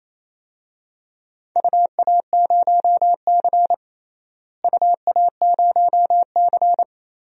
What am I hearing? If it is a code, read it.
Morse code, heard as a keyed tone, about 28 wpm, 700 Hz. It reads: UA0C UA0C
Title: DXCC-100 Call Signs - 4 Characters - Repeated 2x - 28wpm